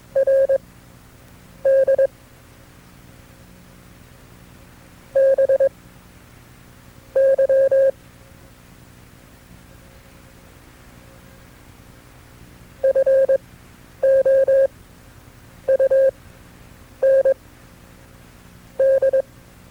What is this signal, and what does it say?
Morse code, heard as a keyed tone, about 21 wpm, spaced as for 7 wpm, 540 Hz. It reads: RD BY FOUND
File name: FreeFiles / CW-practice / KB6MT-tape2-side2